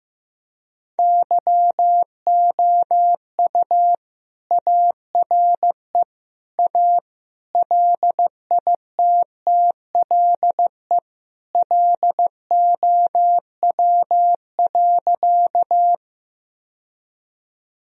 Morse code, heard as a keyed tone, about 15 wpm, 700 Hz. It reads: YOU ARE A LITTLE LOW.
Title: Sentences from Top 1000 Words - Review Part 1 - 15wpm